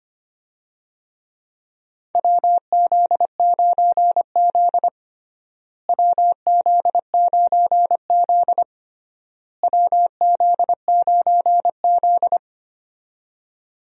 Morse code, heard as a keyed tone, about 25 wpm, 700 Hz. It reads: WZ9Z WZ9Z WZ9Z
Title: USA Call Signs - 4 Characters - Repeated 3x - 25wpm